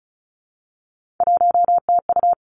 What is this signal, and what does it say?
Morse code, heard as a keyed tone, about 36 wpm, 700 Hz. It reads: 1TU
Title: Three Challenging Letter-Number Combinations 35wpm